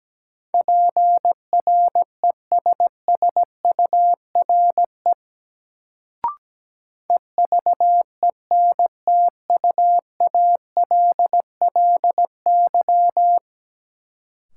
Morse code, heard as a keyed tone, about 17 wpm, 700 Hz. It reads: PRESSURE EVENTUALLY
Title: Sentences from Top 1000 Words - Review Part 6 - New Word Drill - 17wpm